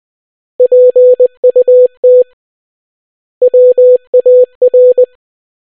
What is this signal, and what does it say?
Morse code, heard as a keyed tone, about 20 wpm, 500 Hz. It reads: PUT WAR